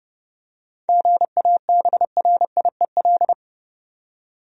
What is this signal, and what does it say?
Morse code, heard as a keyed tone, about 30 wpm, 700 Hz. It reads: GABRIEL
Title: USA Common Names - 30wpm